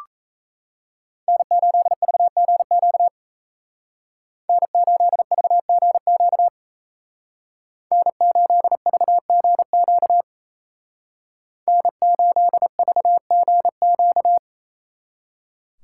Morse code, heard as a keyed tone, about 35 wpm, 700 Hz. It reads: N8VGQ N8VGQ N8VGQ N8VGQ
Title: Speed Racing - USA Call Signs 28wpm